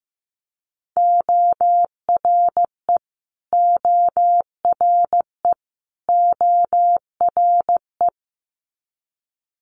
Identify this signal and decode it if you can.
Morse code, heard as a keyed tone, about 15 wpm, 700 Hz. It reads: ORE ORE ORE
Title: Multi-letter Phonemes Repeated 3x 15wpm